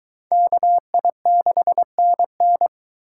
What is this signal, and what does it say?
Morse code, heard as a keyed tone, about 22 wpm, 700 Hz. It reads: KI6NN